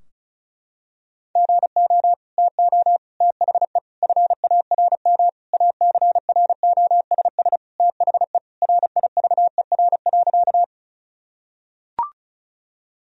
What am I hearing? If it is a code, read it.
Morse code, heard as a keyed tone, about 34 wpm, 700 Hz. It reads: GO TO THE FARM ACROSS THE RIVER.
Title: Sentences from Top 1000 Words - Review Part 2 - 35wpm